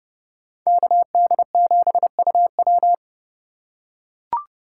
Morse code, heard as a keyed tone, about 32 wpm, 700 Hz. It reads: KD7UW
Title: USA Call Signs - 5 Characters - 3x Character Spacing - 30wpm